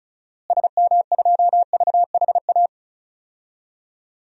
Morse code, heard as a keyed tone, about 36 wpm, 700 Hz. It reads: SM2VHA
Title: DXCC-100 Call Signs - 6 Characters - Repeated 2x - 35wpm